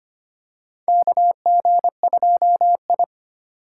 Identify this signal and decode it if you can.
Morse code, heard as a keyed tone, about 25 wpm, 700 Hz. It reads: KG2I
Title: USA Call Signs - 4 Characters - 4x Character Spacing - 25wpm